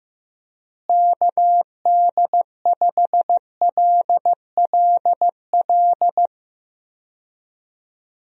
Morse code, heard as a keyed tone, about 15 wpm, 700 Hz. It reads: KD5LLL
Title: USA Call Signs - 6 Characters - Repeated 3x - 15wpm